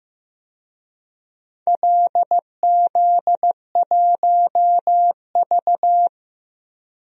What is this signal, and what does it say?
Morse code, heard as a keyed tone, about 15 wpm, 700 Hz. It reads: LZ1V
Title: DXCC-100 Call Signs - 4 Characters - Repeated 2x - 15wpm